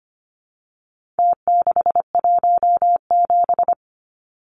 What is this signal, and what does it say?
Morse code, heard as a keyed tone, about 25 wpm, 700 Hz. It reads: T617
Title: Numbers - Contest Style - 4-Digit 25wpm